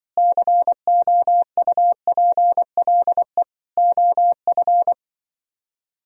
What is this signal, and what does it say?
Morse code, heard as a keyed tone, about 25 wpm, 700 Hz. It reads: COUPLE OF